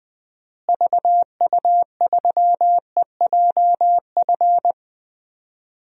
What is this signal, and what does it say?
Morse code, heard as a keyed tone, about 20 wpm, 700 Hz. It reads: VU3EJF
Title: DXCC-100 Call Signs - 6 Characters - Repeated 2x - 20wpm